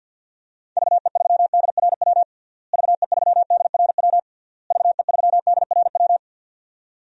Morse code, heard as a keyed tone, about 48 wpm, 700 Hz. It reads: VE3DRW VE3DRW VE3DRW
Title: DXCC-100 Call Signs - 6 Characters - Repeated 3x - 50wpm